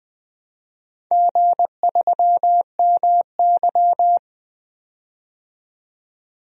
Morse code, heard as a keyed tone, about 20 wpm, 700 Hz. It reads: G3MY
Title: DXCC-100 Call Signs - 4 Characters - 20wpm